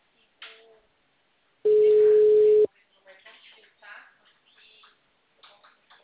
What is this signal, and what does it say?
Ringback tone (Europe, 425 Hz), with speech.